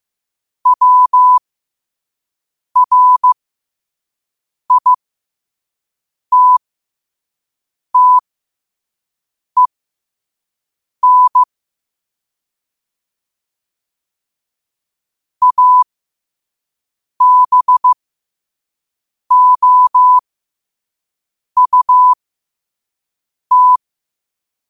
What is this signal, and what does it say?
Morse code, heard as a keyed tone, about 15 wpm, spaced as for 5 wpm, 1000 Hz. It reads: WRITTEN ABOUT
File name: qotdfiles / 5wpm